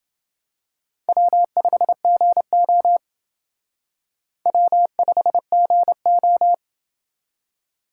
Morse code, heard as a keyed tone, about 28 wpm, 700 Hz. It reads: W5GO W5GO